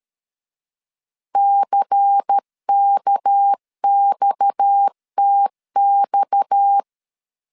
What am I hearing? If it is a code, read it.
Morse code, heard as a keyed tone, about 13 wpm, 800 Hz. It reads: CKXTX
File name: LEZIONI / 6 PKX0 / 6 PKX 12wpm 2024